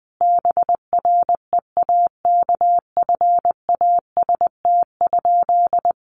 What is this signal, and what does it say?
Morse code, heard as a keyed tone, about 20 wpm, 700 Hz. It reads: BREAKFAST?
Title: Sentences from Top 1500 Words 20wpm